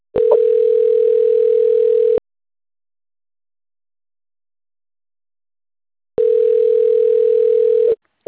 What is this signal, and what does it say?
Ringback tone (North America), 440 + 480 Hz.
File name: rg-1600-unknown- - - .91691